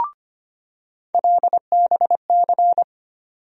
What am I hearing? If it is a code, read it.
Morse code, heard as a keyed tone, about 25 wpm, 700 Hz. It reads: LBC
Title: USA Call Sign Postfixes - 3 Letters 25wpm